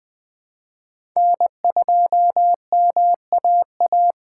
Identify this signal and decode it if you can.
Morse code, heard as a keyed tone, about 20 wpm, 700 Hz. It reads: N2MAA
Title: USA Call Signs - 5 Characters - 2x Character Spacing - 20wpm